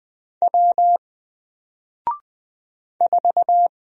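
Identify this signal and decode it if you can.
Morse code, heard as a keyed tone, about 20 wpm, 700 Hz. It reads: W 4
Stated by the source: Single letter and number Morse drill at 20 wpm, from N4LCD.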